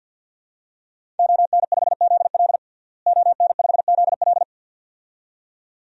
Morse code, heard as a keyed tone, about 53 wpm, 700 Hz. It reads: ON5ZL ON5ZL
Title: DXCC-100 Call Signs - 5 Characters - Repeated 2x - 50wpm